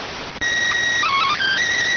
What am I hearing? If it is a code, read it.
SSTV (slow-scan television), PD 290.